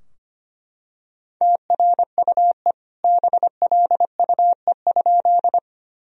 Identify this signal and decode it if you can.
Morse code, heard as a keyed tone, about 25 wpm, 700 Hz. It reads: TRUE BLUE?
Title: Sentences from Top 1000 Words - Review Part 3 - 25wpm